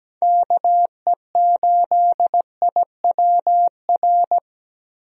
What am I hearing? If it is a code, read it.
Morse code, heard as a keyed tone, about 17 wpm, 700 Hz. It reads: KE8IWR
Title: USA Call Signs - 6 Characters - Repeated 2x - 17wpm